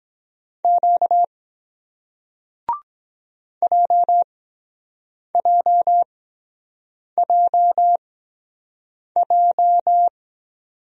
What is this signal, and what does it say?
Morse code, heard as a keyed tone, about 23 wpm, spaced as for 7 wpm, 700 Hz. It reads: Q JJJJ